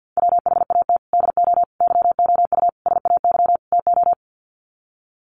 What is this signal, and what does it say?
Morse code, heard as a keyed tone, about 48 wpm, 700 Hz. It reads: WHAT DO YOU SAY TO
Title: Sets of 5 Words 50wpm